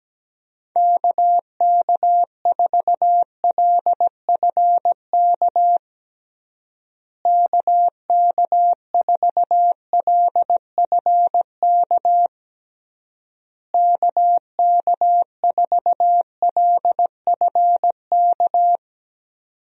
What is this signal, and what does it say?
Morse code, heard as a keyed tone, about 17 wpm, 700 Hz. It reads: KK4LFK KK4LFK KK4LFK
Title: USA Call Signs - 6 Characters - Repeated 3x - 17wpm